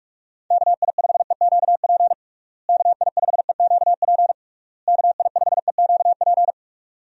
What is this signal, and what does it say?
Morse code, heard as a keyed tone, about 46 wpm, 700 Hz. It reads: KI5EQP KI5EQP KI5EQP
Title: USA Call Signs - 6 Characters - Repeated 3x - 45wpm